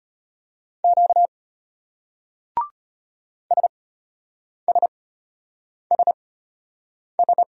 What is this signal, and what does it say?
Morse code, heard as a keyed tone, about 33 wpm, spaced as for 8 wpm, 700 Hz. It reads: Q SSSS